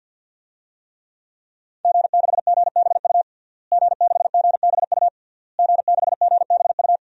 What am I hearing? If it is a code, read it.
Morse code, heard as a keyed tone, about 52 wpm, 700 Hz. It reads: G6GBU G6GBU G6GBU
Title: DXCC-100 Call Signs - 5 Characters - Repeated 3x - 50wpm